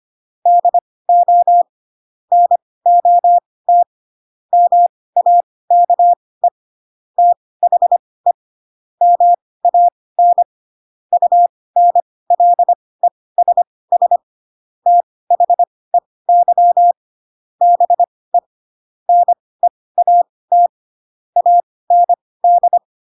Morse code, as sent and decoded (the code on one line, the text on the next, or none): -.. --- / -. --- - / -- .- -.- . / - .... / -- .- -. / ..- -. .-.. ... ... / - .... -.-- / -... / -. .- - / .- -. -..
DO NOT MAKE THE MAN UNLESS THEY BE NEAT AND